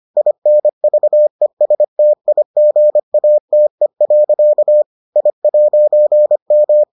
Morse code, heard as a keyed tone, about 25 wpm, 600 Hz. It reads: INVESTIGATE. I'M